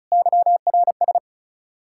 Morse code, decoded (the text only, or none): YRS